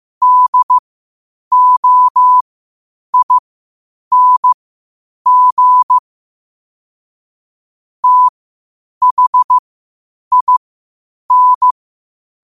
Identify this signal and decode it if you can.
Morse code, heard as a keyed tone, about 15 wpm, spaced as for 8 wpm, 1000 Hz. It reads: DOING THIN
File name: qotdfiles / 8wpm